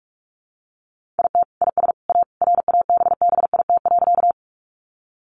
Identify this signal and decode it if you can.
Morse code, heard as a keyed tone, about 44 wpm, 700 Hz. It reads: IT IS A RABBIT.
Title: Easy Phrases 45wpm